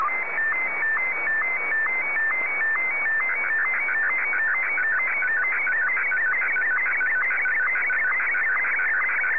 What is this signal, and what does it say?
SSTV (slow-scan television), Martin 1.
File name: M1-VK3